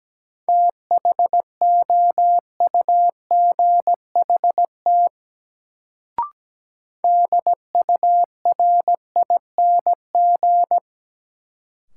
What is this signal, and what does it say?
Morse code, heard as a keyed tone, about 17 wpm, 700 Hz. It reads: THOUGHT DURING
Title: Sets of 2 Words - New Word Drill - 17wpm